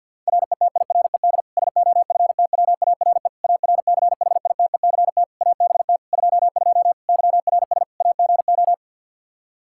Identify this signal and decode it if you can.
Morse code, heard as a keyed tone, about 53 wpm, 700 Hz. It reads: RETIRED SOFTWARE ARCHITECT ABT 22 YRS AGO
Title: Practice QSO - Exchange 3 - More Info - 50wpm